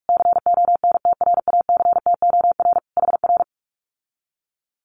Morse code, heard as a keyed tone, about 46 wpm, 700 Hz. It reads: CONTRACTOR HR
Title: Common QSO Phrases Encore 45wpm